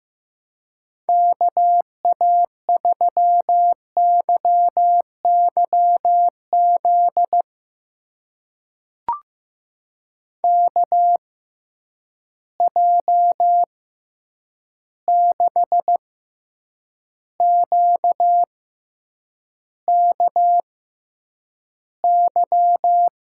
Morse code, heard as a keyed tone, about 15 wpm, spaced as for 8 wpm, 700 Hz. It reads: KA3YYZ KJ6QKY
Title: USA Call Signs - 6 Characters - 6x Character Spacing - 15wpm